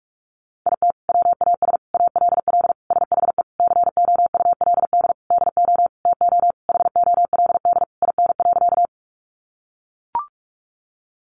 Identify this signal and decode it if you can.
Morse code, heard as a keyed tone, about 44 wpm, 700 Hz. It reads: IT WAS ALL SHE COULD DO TO HOLD IN.
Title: Sentences from Top 400 Words 45wpm